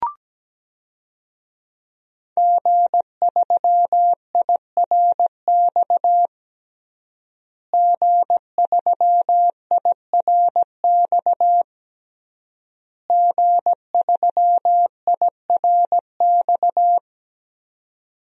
Morse code, heard as a keyed tone, about 17 wpm, 700 Hz. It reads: G3IRX G3IRX G3IRX
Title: DXCC-100 Call Signs - 5 Characters - Repeated 3x - 17wpm